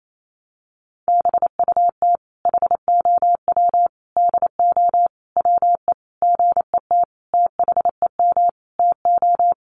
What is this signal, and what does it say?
Morse code, heard as a keyed tone, about 28 wpm, 700 Hz. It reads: BUT HOW DO WE GET THEM TO
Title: Sentences from Top 700 Words 28wpm